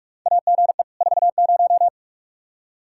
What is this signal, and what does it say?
Morse code, heard as a keyed tone, about 44 wpm, 700 Hz. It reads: AGE 40
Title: Practice QSO - Exchange 3 - More Info - 45wpm